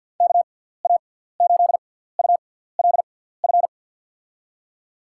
Morse code, heard as a keyed tone, about 52 wpm, spaced as for 17 wpm, 700 Hz. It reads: KA8ULF
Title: USA Call Signs - 6 Characters - 6x Character Spacing - 50wpm